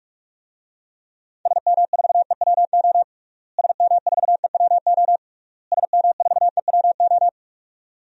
Morse code, heard as a keyed tone, about 46 wpm, 700 Hz. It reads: SM4EWO SM4EWO SM4EWO
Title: DXCC-100 Call Signs - 6 Characters - Repeated 3x - 45wpm